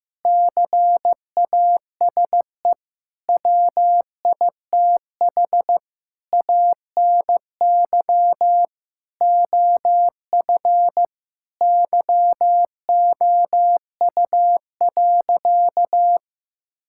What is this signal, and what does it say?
Morse code, heard as a keyed tone, about 15 wpm, 700 Hz. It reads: CASE WITH ANY OF YOU.
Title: Sentences from Top 1600 Words 15wpm